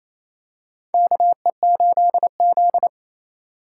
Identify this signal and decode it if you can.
Morse code, heard as a keyed tone, about 28 wpm, 700 Hz. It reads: KE8Z
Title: USA Call Signs - 4x Character Spacing - 28wpm